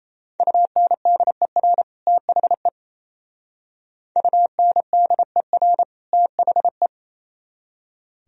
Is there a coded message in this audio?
Yes: Morse code from 0.4 to 6.9 s, heard as a keyed tone, about 37 wpm, 700 Hz. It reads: UNDER THE UNDER THE